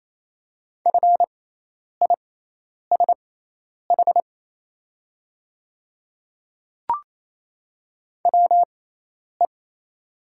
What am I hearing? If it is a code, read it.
Morse code, heard as a keyed tone, about 28 wpm, spaced as for 10 wpm, 690 Hz. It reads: FISH WE
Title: 6x Character Spacing - Top 300 Words - 28wpm